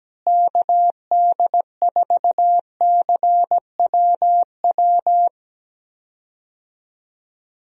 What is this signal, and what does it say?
Morse code, heard as a keyed tone, about 17 wpm, 700 Hz. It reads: KD4CWW
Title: USA Call Signs - 6 Characters - Repeated 3x - 17wpm